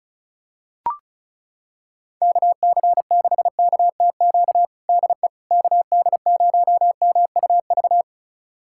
Morse code, heard as a keyed tone, about 36 wpm, 700 Hz. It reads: KC6KTQ DE KD0MUV
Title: USA Callsign DE Callsign 35wpm